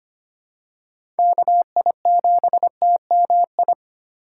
Morse code, heard as a keyed tone, about 25 wpm, 700 Hz. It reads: KI7TMI
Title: USA Call Signs - 6 Characters - 2x Character Spacing - 25wpm